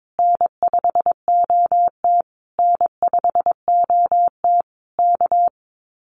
Morse code, heard as a keyed tone, about 22 wpm, 700 Hz. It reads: N5OT N5OT K
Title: Common QSO Phrases Encore 22wpm